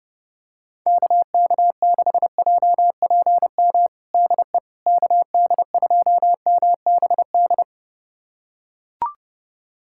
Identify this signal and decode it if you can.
Morse code, heard as a keyed tone, about 30 wpm, 700 Hz. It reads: KK6JPM DE KD2MBD
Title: USA Callsign DE Callsign - 6 Characters - 30wpm